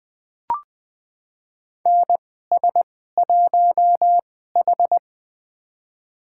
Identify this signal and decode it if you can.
Morse code, heard as a keyed tone, about 20 wpm, spaced as for 15 wpm, 700 Hz. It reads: NS1H
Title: USA Call Signs - 4 Characters - 2x Character Spacing - 20wpm